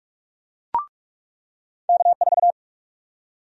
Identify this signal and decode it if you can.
Morse code, heard as a keyed tone, about 46 wpm, 700 Hz. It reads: K4
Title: USA Call Sign Prefixes - 45wpm